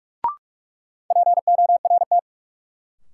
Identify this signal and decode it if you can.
Morse code, heard as a keyed tone, about 46 wpm, 700 Hz. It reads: PORT
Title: Sentences from Top 1000 Words - Review Part 1 - New Word Drill - 45wpm